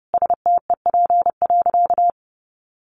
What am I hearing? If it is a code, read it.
Morse code, heard as a keyed tone, about 28 wpm, 700 Hz. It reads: STEP.
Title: Sentences from Top 1400 Words 30wpm